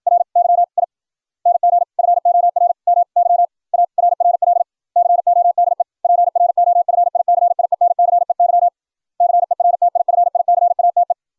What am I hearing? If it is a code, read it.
Morse code, heard as a keyed tone, about 56 wpm, spaced as for 48 wpm, 700 Hz. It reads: RYING FOR MY ARRL CODE PROFICIENCEY CERTIFICATE